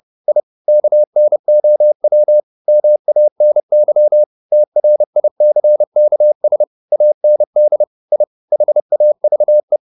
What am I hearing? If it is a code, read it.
Morse code, heard as a keyed tone, about 30 wpm, 600 Hz. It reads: I KNOW MANY TRICKS AND I HAVE
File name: the old man and the sea / the old man and the sea 1 30wpm